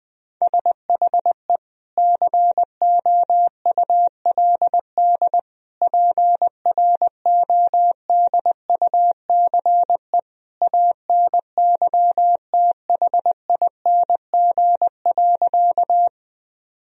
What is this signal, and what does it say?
Morse code, heard as a keyed tone, about 20 wpm, 700 Hz. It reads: SHE COULD PRODUCE ANYTHING.
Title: Sentences from Top 1000 Words - Review Part 3 - 20wpm